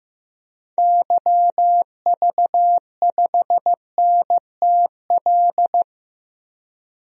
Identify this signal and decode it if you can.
Morse code, heard as a keyed tone, about 15 wpm, 700 Hz. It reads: YV5NTL
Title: DXCC-100 Call Signs - 6 Characters - 15wpm